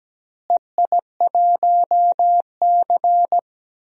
Morse code, heard as a keyed tone, about 17 wpm, 700 Hz. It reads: EI1C
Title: DXCC-100 Call Signs - 4 Characters - Repeated 3x - 17wpm